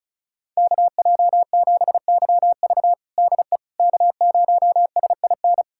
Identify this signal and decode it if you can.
Morse code, heard as a keyed tone, about 36 wpm, 700 Hz. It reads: KJ7YV DE K0SIN
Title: USA Callsign DE Callsign - 5 Characters - 35wpm